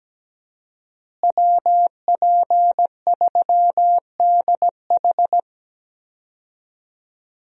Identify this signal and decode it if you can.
Morse code, heard as a keyed tone, about 17 wpm, 700 Hz. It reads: WP3DH
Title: USA Call Signs - 5 Characters - Repeated 2x - 17wpm